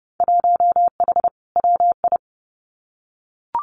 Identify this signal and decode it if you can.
Morse code, heard as a keyed tone, about 31 wpm, 700 Hz. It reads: 1H WI
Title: Winter Field Day Exchanges 30wpm